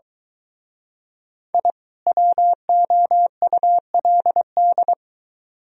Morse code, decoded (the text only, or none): I WOULD